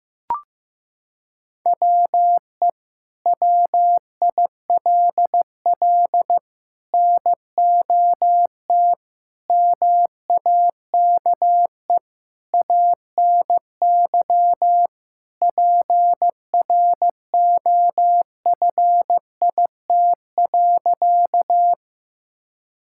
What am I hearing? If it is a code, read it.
Morse code, heard as a keyed tone, about 15 wpm, 700 Hz. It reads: WE WILL NOT MAKE ANY PROFIT.
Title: Sentences from Top 1000 Words - Review Part 5 - 15wpm